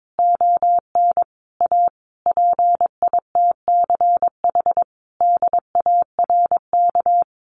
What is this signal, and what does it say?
Morse code, heard as a keyed tone, about 22 wpm, 700 Hz. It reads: ON A PITCH DARK
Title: Sentences from Top 1300 Words 22wpm